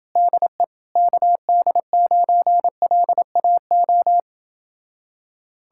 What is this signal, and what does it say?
Morse code, heard as a keyed tone, about 28 wpm, 700 Hz. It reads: DE KD9LAO